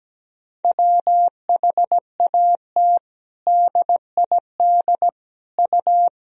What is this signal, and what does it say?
Morse code, heard as a keyed tone, about 17 wpm, 700 Hz. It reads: WHAT DID U
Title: Sentences from Top 100 Words and Abbreviations 17wpm